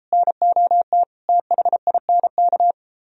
Morse code, heard as a keyed tone, about 33 wpm, 700 Hz. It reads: NOT THINK